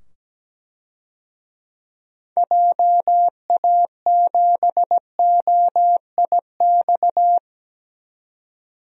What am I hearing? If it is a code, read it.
Morse code, heard as a keyed tone, about 17 wpm, 700 Hz. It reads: JA7OIX
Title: DXCC-100 Call Signs - 6 Characters - 17wpm